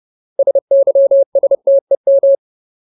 Morse code, heard as a keyed tone, about 30 wpm, 550 Hz. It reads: SYSTEM
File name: Ultimate Word List / 30WPM Ultimate